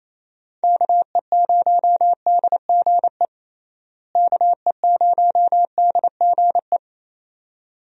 Morse code, heard as a keyed tone, about 28 wpm, 700 Hz. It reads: KE0DGE KE0DGE
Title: USA Call Signs - 6 Characters - Repeated 2x - 28wpm